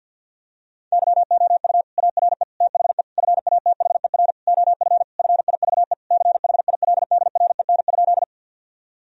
Morse code, heard as a keyed tone, about 50 wpm, 700 Hz. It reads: YOU ARE THE FATHER OF FIVE CHILDREN?
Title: Sentences from Top 1000 Words - Review Part 2 - 50wpm